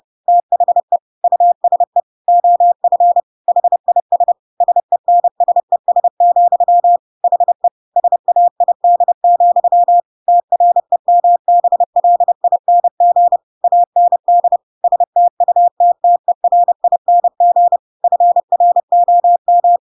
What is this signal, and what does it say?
Morse code, heard as a keyed tone, about 30 wpm, 700 Hz. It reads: THE USE OF HIS SENSES, HE SAID, TREMBLING AND STUTTERING FROM
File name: Pinocchio / Pinocchio ch1 30 30 ik1hge